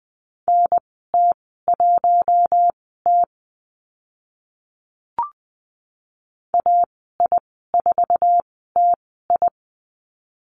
Morse code, heard as a keyed tone, about 20 wpm, spaced as for 15 wpm, 700 Hz. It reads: NT1T AI4TI
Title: 048 Lesson 9g - Number 4 - Call Signs 20wpm